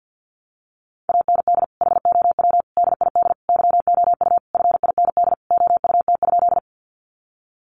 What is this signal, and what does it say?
Morse code, heard as a keyed tone, about 50 wpm, 700 Hz. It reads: AND HOW DID YOU FIND OUT?
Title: Sentences from Top 200 Words 50wpm